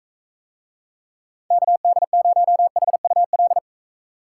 Morse code, heard as a keyed tone, about 42 wpm, 700 Hz. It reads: KD0HUL